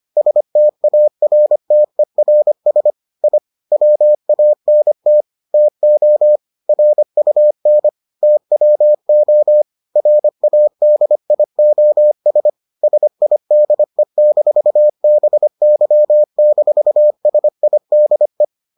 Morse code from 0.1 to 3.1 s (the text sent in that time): STARTERS